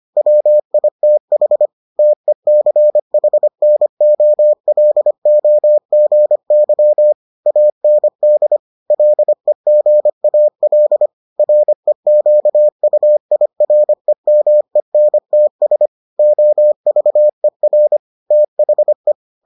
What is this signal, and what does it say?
Morse code, heard as a keyed tone, about 25 wpm, 600 Hz. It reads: WITH TECHNOLOGY AND LEGAL REQUIREMENTS OVER THE